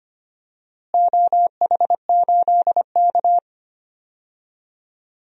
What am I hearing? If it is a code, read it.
Morse code, heard as a keyed tone, about 25 wpm, 700 Hz. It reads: OH8K
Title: DXCC-100 Call Signs - 4 Characters - Repeated 3x - 25wpm